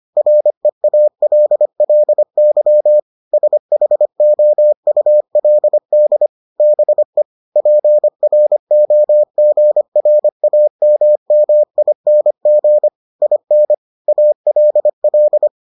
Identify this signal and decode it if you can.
Morse code, heard as a keyed tone, about 25 wpm, 600 Hz. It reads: REALLY SHOULD BE PROGRAMMING IN ALL